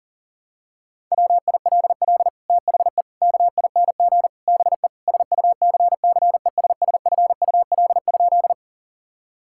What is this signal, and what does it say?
Morse code, heard as a keyed tone, about 37 wpm, 700 Hz. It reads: WILL THE KING BE SUCCESSFUL?
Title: Sentences from Top 1000 Words - Review Part 6 - 40wpm